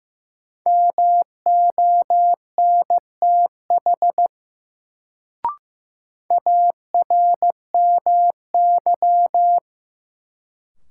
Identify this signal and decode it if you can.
Morse code, heard as a keyed tone, about 15 wpm, 700 Hz. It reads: MONTH ARMY
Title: Sentences from Top 800 Words - New Word Drill - 15wpm